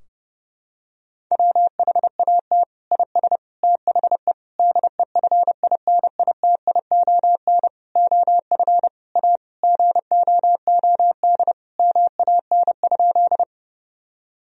Morse code, heard as a keyed tone, about 30 wpm, 700 Hz. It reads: WHAT IS THE DEFINITION OF A GOOD MAN?
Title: Sentences from Top 1000 Words - Review Part 4 - 30wpm